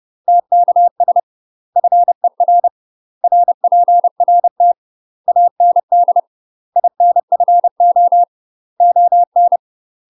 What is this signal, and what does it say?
Morse code, heard as a keyed tone, about 30 wpm, 700 Hz. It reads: TKS FER RPRT AND INFO ON
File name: UK MASTER QSOs TRAINING / 30WPM / UK MASTER QSOs 0048